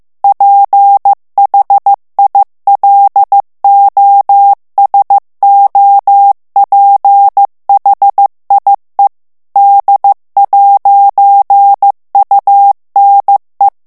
Morse code, heard as a keyed tone, about 15 wpm, 800 Hz. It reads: PHILOSOPHIE D'UNE
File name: FRANCAIS TEXTE CLAIR 2 VITESSE 14 MOTS MINUTES